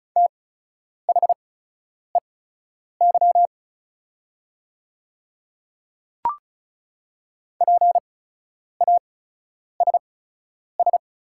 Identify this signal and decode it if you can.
Morse code, heard as a keyed tone, about 36 wpm, spaced as for 10 wpm, 700 Hz. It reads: THEY PASS
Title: 8x Character Spacing - Top 500 Words - 35wpm